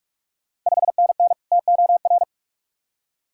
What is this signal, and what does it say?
Morse code, heard as a keyed tone, about 46 wpm, 700 Hz. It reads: 5NN TOR
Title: Ontario QSO Party Exchange 45wpm